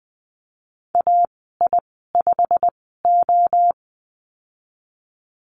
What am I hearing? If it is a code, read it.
Morse code, heard as a keyed tone, about 19 wpm, spaced as for 14 wpm, 700 Hz. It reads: AI5O
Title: 147 Lesson 24g - Letter y - Call Signs 20wpm